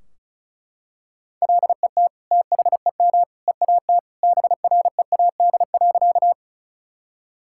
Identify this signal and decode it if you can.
Morse code, heard as a keyed tone, about 34 wpm, 700 Hz. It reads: LET THEM EAT BREAD.
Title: Sentences from Top 1600 Words 35wpm